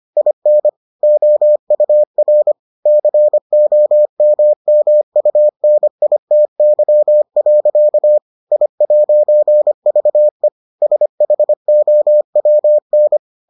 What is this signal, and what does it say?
Morse code, heard as a keyed tone, about 25 wpm, 600 Hz. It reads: IN OUR COMMUNITY. I'VE SHOWN